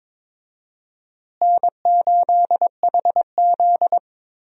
Morse code, heard as a keyed tone, about 22 wpm, 700 Hz. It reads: N8HZ
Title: USA Call Signs - 4 Characters - Repeated 3x - 22wpm